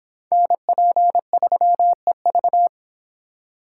Morse code, heard as a keyed tone, about 25 wpm, 700 Hz. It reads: NP3EV